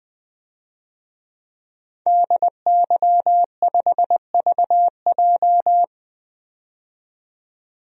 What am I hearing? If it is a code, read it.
Morse code, heard as a keyed tone, about 20 wpm, 700 Hz. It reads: DY5VJ
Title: DXCC-100 Call Signs - 5 Characters - 20wpm